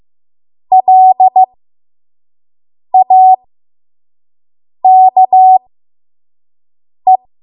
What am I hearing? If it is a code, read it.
Morse code, heard as a keyed tone, about 15 wpm, spaced as for 5 wpm, 750 Hz. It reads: LAKE
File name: Archive / 5 WPM / 05WPM